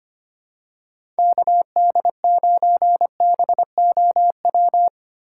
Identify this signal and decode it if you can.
Morse code, heard as a keyed tone, about 25 wpm, 700 Hz. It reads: KD9BOW
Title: USA Call Signs - 6 Characters - 3x Character Spacing - 25wpm